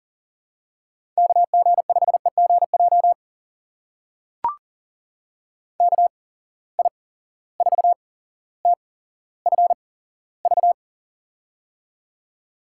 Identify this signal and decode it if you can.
Morse code, heard as a keyed tone, about 38 wpm, spaced as for 17 wpm, 700 Hz. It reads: KG5EGJ KI4TFV
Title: USA Call Signs - 8x Character Spacing - 40wpm